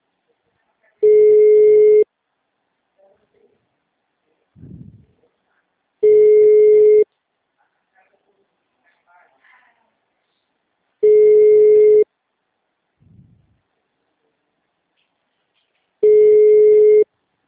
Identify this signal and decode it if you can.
Ringback tone (Europe, 425 Hz), with speech.